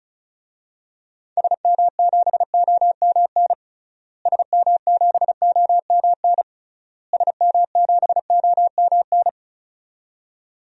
Morse code, heard as a keyed tone, about 34 wpm, 700 Hz. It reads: SM7OMN SM7OMN SM7OMN
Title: DXCC-100 Call Signs - 6 Characters - Repeated 3x - 35wpm